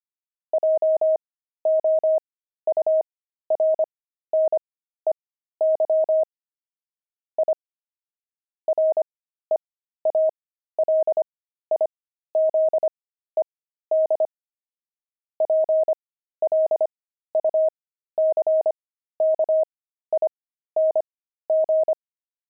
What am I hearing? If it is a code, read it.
Morse code, heard as a keyed tone, about 25 wpm, spaced as for 13 wpm, 630 Hz. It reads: JOURNEY I REALIZED PLUCKING
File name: pangram-fs-13-cs-25-journey